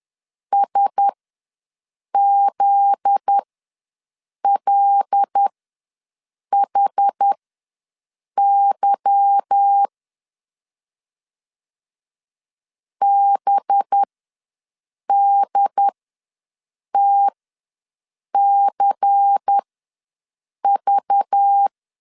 Morse code, heard as a keyed tone, about 10 wpm, spaced as for 6 wpm, 800 Hz. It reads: SZLHY BDTCV
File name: LEZIONI / 8 YJ45 / 8 YJ45 ws 10wpm 3843